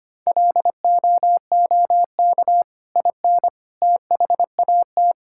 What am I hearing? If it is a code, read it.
Morse code, heard as a keyed tone, about 25 wpm, 700 Hz. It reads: LOOK IN THAT
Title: Sentences from Top 1000 Words - Review Part 5 - 25wpm